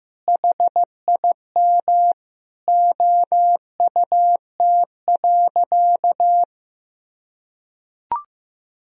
Morse code, heard as a keyed tone, about 16 wpm, 700 Hz. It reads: HIM OUT.
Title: Sentences from Top 1000 Words - Review Part 3 - 15wpm